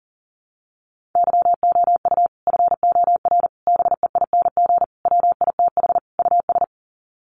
Morse code, heard as a keyed tone, about 40 wpm, 700 Hz. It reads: YOU FOR BEING WITH US
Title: Sets of 5 Words 40wpm